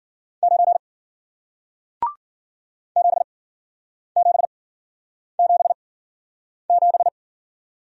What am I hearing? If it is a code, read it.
Morse code, heard as a keyed tone, about 53 wpm, spaced as for 9 wpm, 700 Hz. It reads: 9 7777